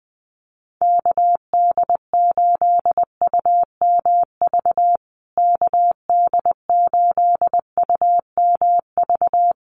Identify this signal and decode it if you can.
Morse code, heard as a keyed tone, about 20 wpm, 700 Hz. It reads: KD8UMV KD8UMV
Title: USA Call Signs Repeated 2x 20wpm